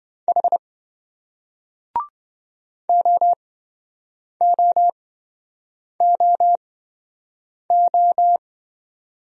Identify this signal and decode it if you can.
Morse code, heard as a keyed tone, about 27 wpm, spaced as for 7 wpm, 700 Hz. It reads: H OOOO